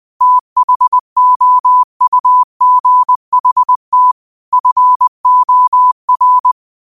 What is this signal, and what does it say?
Morse code, heard as a keyed tone, about 20 wpm, 1000 Hz. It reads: THOUGHT FOR
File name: qotdfiles / 20wpm